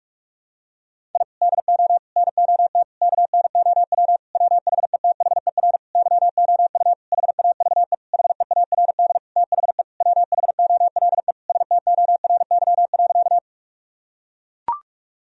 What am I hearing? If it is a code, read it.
Morse code, heard as a keyed tone, about 41 wpm, 700 Hz. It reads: I DO NOT KNOW WHETHER YOU HAVE HEARD THE WHOLE STORY.
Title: Sentences from Top 1000 Words - Review Part 3 - 45wpm